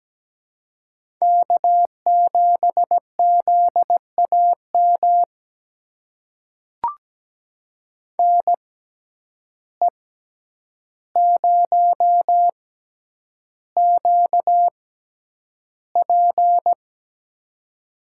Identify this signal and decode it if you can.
Morse code, heard as a keyed tone, about 17 wpm, spaced as for 9 wpm, 700 Hz. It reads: K7ZAM NE0QP
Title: USA Call Signs - 5 Characters - 6x Character Spacing - 17wpm